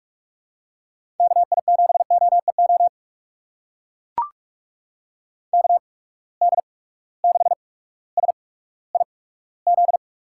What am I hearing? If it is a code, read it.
Morse code, heard as a keyed tone, about 42 wpm, spaced as for 19 wpm, 700 Hz. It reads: KI7OEO KD6SIZ
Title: USA Call Signs - 6 Characters - 8x Character Spacing - 45wpm